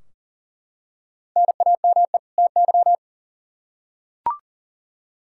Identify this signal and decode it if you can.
Morse code, heard as a keyed tone, about 40 wpm, 700 Hz. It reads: NAME TY
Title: Practice QSO - Exchange 1 - Beginning - 40wpm